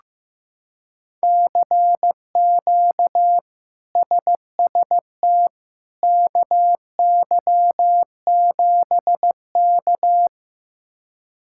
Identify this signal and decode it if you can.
Morse code, heard as a keyed tone, about 15 wpm, 700 Hz. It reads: CQ SST KY7K
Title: K1USN SST QSO Practice - 15wpm